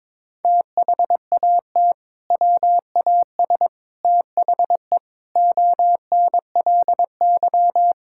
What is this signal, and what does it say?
Morse code, heard as a keyed tone, about 22 wpm, 700 Hz. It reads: THAT WAS THE ONLY
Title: Sentences from Top 1600 Words 22wpm